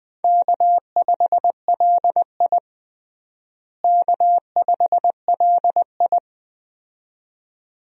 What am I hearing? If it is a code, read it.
Morse code, heard as a keyed tone, about 20 wpm, 700 Hz. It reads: K5LI K5LI
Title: USA Call Signs - 4 Characters - Repeated 2x - 20wpm